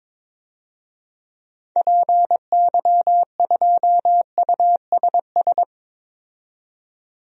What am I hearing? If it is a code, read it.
Morse code, heard as a keyed tone, about 22 wpm, 700 Hz. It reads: PY2USS
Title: DXCC-100 Call Signs - 6 Characters - 22wpm